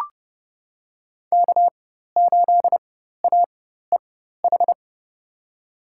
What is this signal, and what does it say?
Morse code, heard as a keyed tone, about 32 wpm, spaced as for 14 wpm, 700 Hz. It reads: K8AEH
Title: USA Call Signs - 5 Characters - 4x Character Spacing - 30wpm